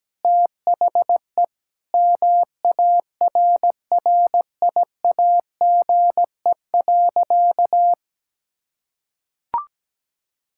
Morse code, heard as a keyed tone, about 17 wpm, 700 Hz. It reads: THE MARRIAGE.
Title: Sentences from Top 1000 Words - Review Part 6 - 17wpm